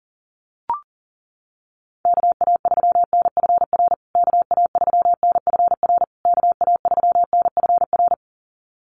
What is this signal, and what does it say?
Morse code, heard as a keyed tone, about 40 wpm, 700 Hz. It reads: KA3NFR KA3NFR KA3NFR
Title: USA Call Signs Repeated 3x 40wpm